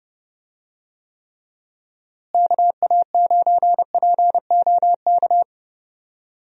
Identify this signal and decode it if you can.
Morse code, heard as a keyed tone, about 30 wpm, 700 Hz. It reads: KA9POK
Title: USA Call Signs - 6 Characters - Repeated 3x - 30wpm